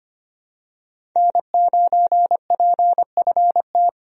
Morse code, heard as a keyed tone, about 25 wpm, 700 Hz. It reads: N9PFT
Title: USA Call Signs - 5 Characters - 25wpm